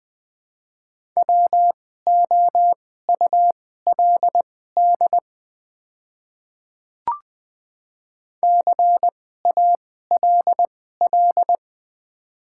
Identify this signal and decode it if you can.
Morse code, heard as a keyed tone, about 20 wpm, spaced as for 14 wpm, 700 Hz. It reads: WOULD CALL
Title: 2x Character Spacing - Top 100 Words - 20wpm